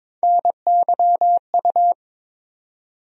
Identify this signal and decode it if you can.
Morse code, heard as a keyed tone, about 22 wpm, 700 Hz. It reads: NYU
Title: USA Call Sign Postfixes - 3 Letters 22wpm